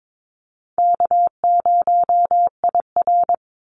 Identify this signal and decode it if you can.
Morse code, heard as a keyed tone, about 22 wpm, 700 Hz. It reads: K0IR
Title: Call Signs - 22wpm